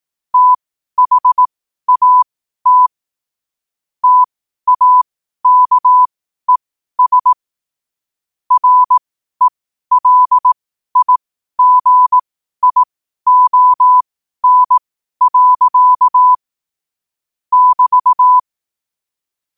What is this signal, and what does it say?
Morse code, heard as a keyed tone, about 18 wpm, spaced as for 12 wpm, 1000 Hz. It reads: THAT TAKES RELIGION. <BT>
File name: qotdfiles / 12wpm